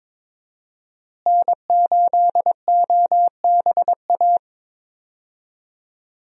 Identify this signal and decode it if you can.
Morse code, heard as a keyed tone, about 22 wpm, 700 Hz. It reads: N8OBA
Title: USA Call Signs - 5 Characters - Repeated 3x - 22wpm